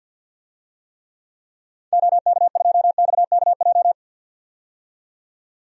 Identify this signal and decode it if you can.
Morse code, heard as a keyed tone, about 50 wpm, 700 Hz. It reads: OK2XKJ
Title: DXCC-100 Call Signs - 6 Characters - 50wpm